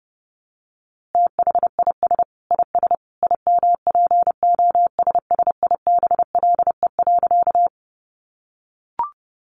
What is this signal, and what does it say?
Morse code, heard as a keyed tone, about 30 wpm, 700 Hz. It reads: THIS IS IMPOSSIBLE.
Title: A Sound Of Thunder 30wpm